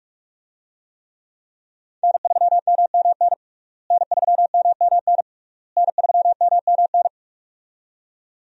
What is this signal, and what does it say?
Morse code, heard as a keyed tone, about 46 wpm, 700 Hz. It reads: N3MMN N3MMN N3MMN
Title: USA Call Signs - 5 Characters - Repeated 3x - 45wpm